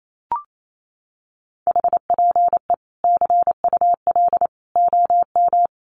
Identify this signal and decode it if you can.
Morse code, heard as a keyed tone, about 28 wpm, 700 Hz. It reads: HPE CUL OM
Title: Common QSO Phrases 28wpm